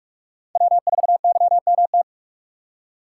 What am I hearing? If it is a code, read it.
Morse code, heard as a keyed tone, about 42 wpm, 700 Hz. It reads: W4YMT
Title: USA Call Signs - 5 Characters - 4x Character Spacing - 45wpm